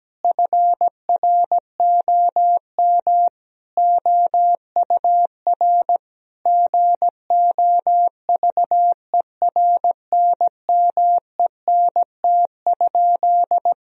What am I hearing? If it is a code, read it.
Morse code, heard as a keyed tone, about 17 wpm, 700 Hz. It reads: FROM OUR GOVERNMENT?
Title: Sentences from Top 1000 Words - Review Part 3 - 17wpm